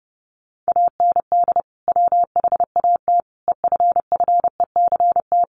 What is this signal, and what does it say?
Morse code, heard as a keyed tone, about 30 wpm, 700 Hz. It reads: AND WHAT EFFECT
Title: Sentences from Top 1000 Words 30wpm